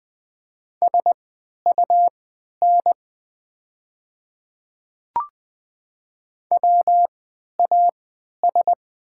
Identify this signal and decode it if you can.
Morse code, heard as a keyed tone, about 20 wpm, spaced as for 11 wpm, 700 Hz. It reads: SUN WAS
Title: 3x Character Spacing - Top 300 Words - 20wpm